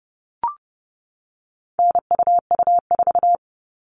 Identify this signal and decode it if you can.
Morse code, heard as a keyed tone, about 30 wpm, 700 Hz. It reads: NUU4
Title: All Cut Numbers - 4-Digit 30wpm